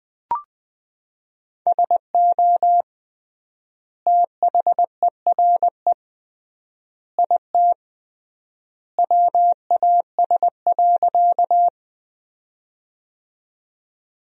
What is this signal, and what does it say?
Morse code, heard as a keyed tone, about 20 wpm, 700 Hz. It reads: SO THERE IT WAS.
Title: 3x Word-Spacing - Sentences from Top 100 Words - 20wpm